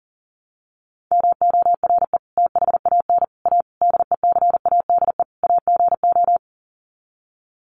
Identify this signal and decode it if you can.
Morse code, heard as a keyed tone, about 40 wpm, 700 Hz. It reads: MORE THAN A DECADE AGO
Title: Sets of 5 Words Encore2 40wpm